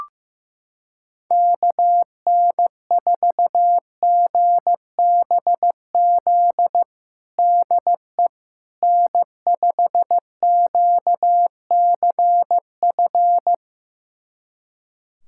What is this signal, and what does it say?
Morse code, heard as a keyed tone, about 15 wpm, 700 Hz. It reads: KN4GBZ DE N5QCF
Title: USA Callsign DE Callsign 15wpm